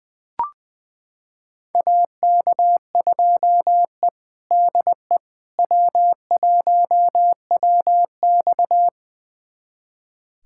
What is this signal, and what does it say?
Morse code, heard as a keyed tone, about 20 wpm, 700 Hz. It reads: AK2E DE W1WX
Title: USA Callsign DE Callsign - 4 Characters - 20wpm